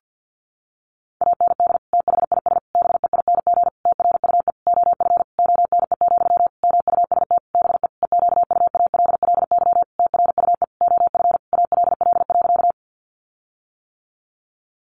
Morse code, heard as a keyed tone, about 50 wpm, 700 Hz. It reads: AND THIS BEING TRUE OF ONE, MUST BE EQUALLY TRUE OF ALL.
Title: Sentences from Top 1400 Words 50wpm